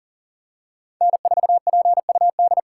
Morse code, heard as a keyed tone, about 40 wpm, 690 Hz. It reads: N4PUD
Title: USA Call Signs - 8x Character Spacing - 40wpm